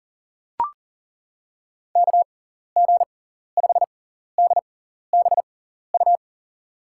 Morse code, heard as a keyed tone, about 40 wpm, spaced as for 14 wpm, 700 Hz. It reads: KG5DBU
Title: USA Call Signs - 6 Characters - 6x Character Spacing - 40wpm